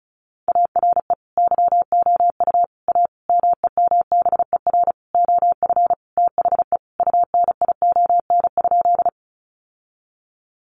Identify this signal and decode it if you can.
Morse code, heard as a keyed tone, about 34 wpm, 700 Hz. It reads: ARE YOU A MEMBER OF THE UNION?
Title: Sentences from Top 1400 Words 35wpm